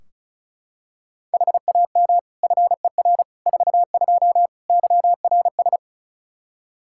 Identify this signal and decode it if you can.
Morse code, heard as a keyed tone, about 34 wpm, 700 Hz. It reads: HAM FER 42 YRS
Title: Practice QSO - Exchange 3 - More Info - 35wpm